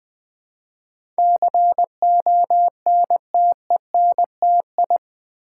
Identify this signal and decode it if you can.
Morse code, heard as a keyed tone, about 20 wpm, 700 Hz. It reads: CONTENTI
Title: Build A Word - 4 Letters At A Time - 10-Letter Words - 20wpm